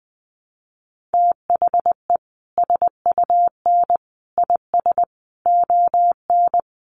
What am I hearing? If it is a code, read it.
Morse code, heard as a keyed tone, about 20 wpm, 700 Hz. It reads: THE SUN IS ON
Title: Sentences from Top 1300 Words 20wpm